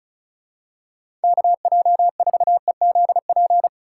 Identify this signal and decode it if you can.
Morse code, heard as a keyed tone, about 34 wpm, 700 Hz. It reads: KJ4EZP
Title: USA Call Signs - 6 Characters - 3x Character Spacing - 35wpm